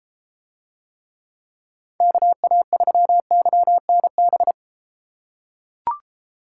Morse code, heard as a keyed tone, about 33 wpm, 700 Hz. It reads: KA3YNB